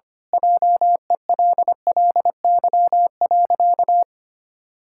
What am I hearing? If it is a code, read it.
Morse code, heard as a keyed tone, about 25 wpm, 700 Hz. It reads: JELLY.
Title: Pangram Challenge v2 - 25wpm